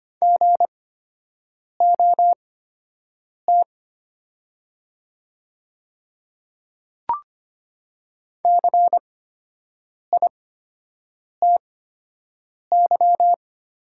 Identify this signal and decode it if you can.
Morse code, heard as a keyed tone, about 25 wpm, spaced as for 7 wpm, 700 Hz. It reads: GOT CITY
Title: 8x Character Spacing - Top 500 Words - 25wpm